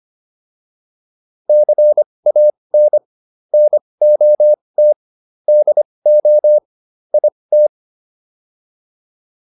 Morse code, heard as a keyed tone, about 25 wpm, spaced as for 20 wpm, 600 Hz. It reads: CAN NOT DO IT